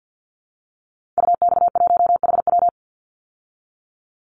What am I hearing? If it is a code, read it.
Morse code, heard as a keyed tone, about 50 wpm, 700 Hz. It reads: UX1HW
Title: Call Signs Repeated 2x - 50wpm